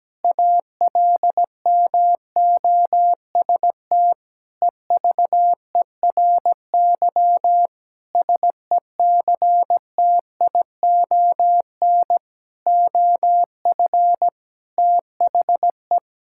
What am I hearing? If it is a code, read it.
Morse code, heard as a keyed tone, about 17 wpm, 700 Hz. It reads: ALMOST EVERY SECTION OF THE
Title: Sentences from Top 1000 Words - Review Part 5 - 17wpm